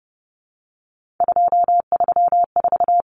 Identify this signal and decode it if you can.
Morse code, heard as a keyed tone, about 30 wpm, 700 Hz. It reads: 234
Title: Numbers - 3-Digit 30wpm